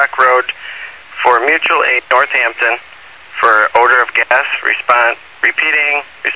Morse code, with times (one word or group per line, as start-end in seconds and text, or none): none